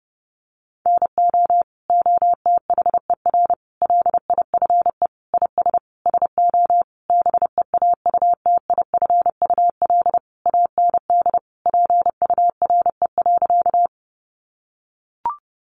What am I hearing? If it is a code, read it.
Morse code, heard as a keyed tone, about 30 wpm, 700 Hz. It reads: NO OTHER LIFE IS SO BEAUTIFUL AND PURE.
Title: Sentences from Top 1500 Words 30wpm